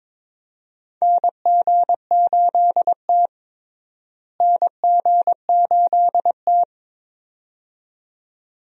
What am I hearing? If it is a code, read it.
Morse code, heard as a keyed tone, about 22 wpm, 700 Hz. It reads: NG8T NG8T
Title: USA Call Signs - 4 Characters - Repeated 2x - 22wpm